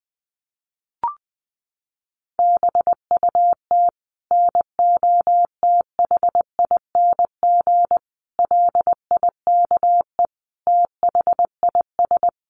Morse code, heard as a keyed tone, about 20 wpm, 700 Hz. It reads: BUT NOTHING LIKE THIS
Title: Sentences from Top 1100 Words 20wpm